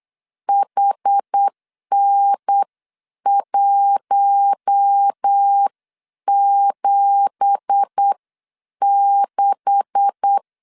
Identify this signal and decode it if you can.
Morse code, heard as a keyed tone, about 8 wpm, spaced as for 7 wpm, 800 Hz. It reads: HN176
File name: CW / DET / 1 9 1500C S C 8wpm 5653